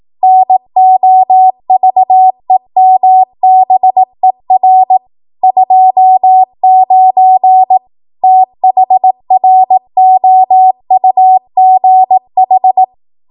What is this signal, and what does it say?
Morse code, heard as a keyed tone, about 18 wpm, 750 Hz. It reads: NOVEMBER 29 THROUGH
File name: Archive / 18 WPM / 18WPM